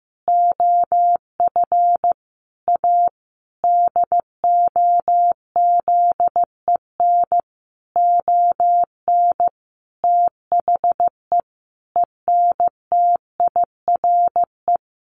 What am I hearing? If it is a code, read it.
Morse code, heard as a keyed tone, about 15 wpm, 700 Hz. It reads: OF A DOZEN ON THE ENTIRE
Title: A Sound Of Thunder 15wpm